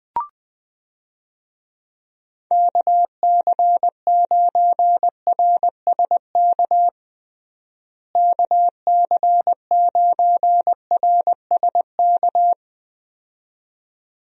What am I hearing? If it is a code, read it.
Morse code, heard as a keyed tone, about 20 wpm, 700 Hz. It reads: KC9RSK KC9RSK
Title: USA Call Signs - 6 Characters - Repeated 2x - 20wpm